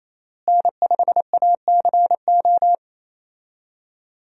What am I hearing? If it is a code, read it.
Morse code, heard as a keyed tone, about 28 wpm, 700 Hz. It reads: N5ACO
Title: USA Call Signs - 5 Characters - Repeated 3x - 28wpm